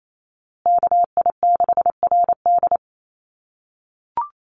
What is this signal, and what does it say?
Morse code, heard as a keyed tone, about 28 wpm, 700 Hz. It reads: KI6RD
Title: USA Call Signs Repeated 2x 28wpm